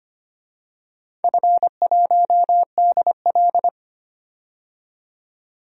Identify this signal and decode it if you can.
Morse code, heard as a keyed tone, about 25 wpm, 690 Hz. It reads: F1DL
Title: DXCC-100 Call Signs - 4 Characters - 25wpm